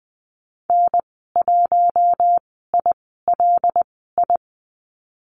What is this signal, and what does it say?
Morse code, heard as a keyed tone, about 20 wpm, spaced as for 15 wpm, 700 Hz. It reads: N1ILI
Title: 122 Lesson 20g - Number 3 - Call Signs 20wpm